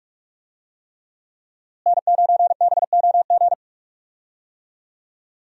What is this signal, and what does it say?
Morse code, heard as a keyed tone, about 42 wpm, 700 Hz. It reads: N9BOG
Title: USA Call Signs - 5 Characters - Repeated 3x - 45wpm